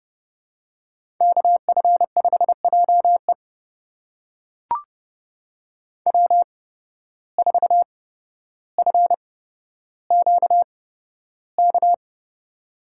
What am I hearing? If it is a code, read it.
Morse code, heard as a keyed tone, about 30 wpm, spaced as for 13 wpm, 700 Hz. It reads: KF5JE W4FQK
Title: USA Call Signs - 5 Characters - 8x Character Spacing - 30wpm